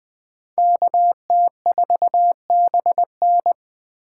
Morse code, heard as a keyed tone, about 25 wpm, 700 Hz. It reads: KT4BN
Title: Speed Racing - USA Call Signs 20wpm